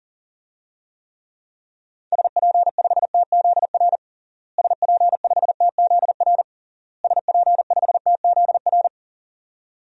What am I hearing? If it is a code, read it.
Morse code, heard as a keyed tone, about 37 wpm, 700 Hz. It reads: SP5TZR SP5TZR SP5TZR
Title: DXCC-100 Call Signs - 6 Characters - Repeated 3x - 40wpm